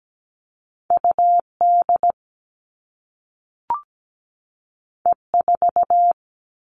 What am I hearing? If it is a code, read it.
Morse code, heard as a keyed tone, about 17 wpm, 700 Hz. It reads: UD E4
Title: All Cut Numbers - 2-Digit 17wpm